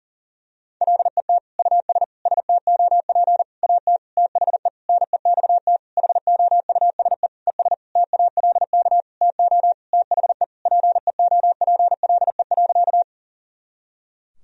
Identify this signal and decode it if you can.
Morse code, heard as a keyed tone, about 40 wpm, 700 Hz. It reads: LET US STOP AT THE NEXT HOUSE ES TALK TO THE PEOPLE.
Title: Sentences from Top 400 Words and Abbreviations 40wpm